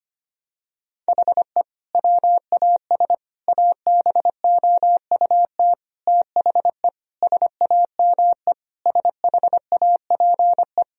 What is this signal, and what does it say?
Morse code, heard as a keyed tone, about 25 wpm, 700 Hz. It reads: HE WAS ABOUT THE SAME SHAPE
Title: Sentences from Top 1000 Words - Review Part 3 - 25wpm